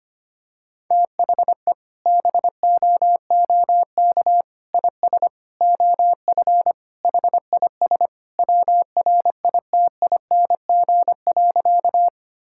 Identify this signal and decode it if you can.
Morse code, heard as a keyed tone, about 25 wpm, 690 Hz. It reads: THE BOOK IS OF HIS WRITING.
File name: practice / 25wpm / Sentences with 2-Word Phrases - 25wpm